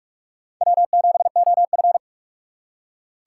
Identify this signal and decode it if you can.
Morse code, heard as a keyed tone, about 46 wpm, 700 Hz. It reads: W7OF
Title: USA Call Signs - 4 Characters - 3x Character Spacing - 45wpm